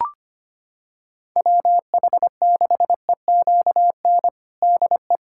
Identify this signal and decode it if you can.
Morse code, heard as a keyed tone, about 25 wpm, 700 Hz. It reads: WH6EQN DE
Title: USA Callsign DE Callsign - 6 Characters - 25wpm